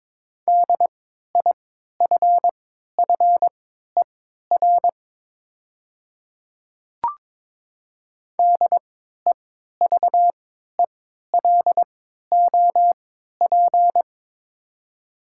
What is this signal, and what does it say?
Morse code, heard as a keyed tone, about 22 wpm, spaced as for 12 wpm, 700 Hz. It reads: DIFFER DEVELOP
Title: 3x Character Spacing - Top 500 Words - 22wpm